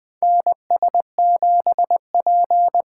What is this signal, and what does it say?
Morse code, heard as a keyed tone, about 21 wpm, 700 Hz. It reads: NS7P
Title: 2023 SOTA Chaser Call Signs - 20wpm